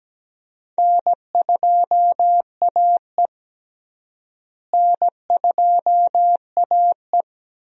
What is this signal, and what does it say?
Morse code, heard as a keyed tone, about 17 wpm, 700 Hz. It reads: N2AE N2AE
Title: USA Call Signs - 4 Characters - Repeated 3x - 17wpm